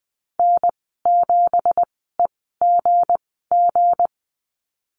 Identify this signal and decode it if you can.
Morse code, heard as a keyed tone, about 20 wpm, spaced as for 14 wpm, 700 Hz. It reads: N7EGG
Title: 173 Lesson 28g - Number 9 - Call Signs 20wpm